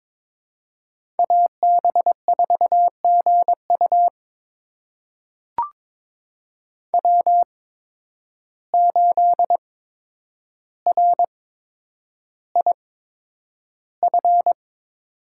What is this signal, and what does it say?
Morse code, heard as a keyed tone, about 21 wpm, spaced as for 9 wpm, 700 Hz. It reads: AB4GU W8RIF
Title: USA Call Signs - 5 Characters - 8x Character Spacing - 22wpm